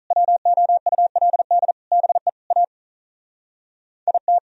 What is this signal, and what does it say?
Morse code, heard as a keyed tone, about 47 wpm, 700 Hz. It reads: WOULD BE A IT